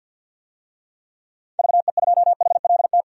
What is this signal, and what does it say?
Morse code, heard as a keyed tone, about 52 wpm, 700 Hz. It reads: VE2HLT
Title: DXCC-100 Call Signs - 6 Characters - 50wpm